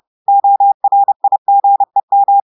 Morse code, heard as a keyed tone, about 30 wpm, 800 Hz. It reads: ORIGEM
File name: audios / portugais / portugais texte en clair 1 vitesse 30 mots minutes